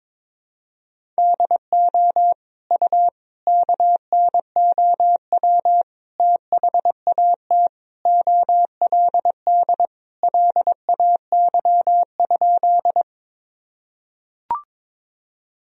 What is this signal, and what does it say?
Morse code, heard as a keyed tone, about 22 wpm, 700 Hz. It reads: DO U KNOW THAT OLD LAY?
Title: Sentences from Top 400 Words and Abbreviations 22wpm